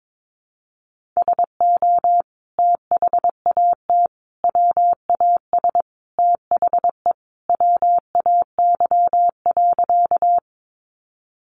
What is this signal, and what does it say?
Morse code, heard as a keyed tone, about 22 wpm, 700 Hz. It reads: SO THAT WAS THE WAY.
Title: Sentences from Top 100 Words 22wpm